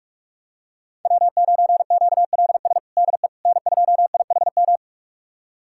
Morse code, heard as a keyed tone, about 43 wpm, 700 Hz. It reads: W9QLS DE N2IHM
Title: USA Callsign DE Callsign - 5 Characters - 45wpm